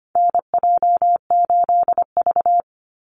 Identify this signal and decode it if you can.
Morse code, heard as a keyed tone, about 25 wpm, 700 Hz. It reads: NJ8V
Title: Call Signs - 25wpm